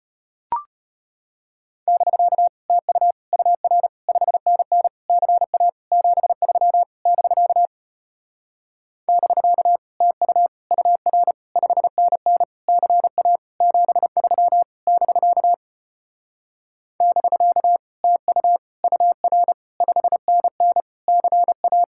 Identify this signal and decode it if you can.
Morse code, heard as a keyed tone, about 33 wpm, 700 Hz. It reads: <BK> TU UR 5NN CA 73 <BK> <BK> TU UR 5NN CA 73 <BK> <BK> TU UR 5NN CA